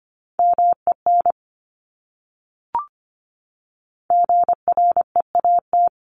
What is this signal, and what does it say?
Morse code, heard as a keyed tone, about 25 wpm, 700 Hz. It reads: MEN GREAT
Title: Top 300 Words Repeated 25wpm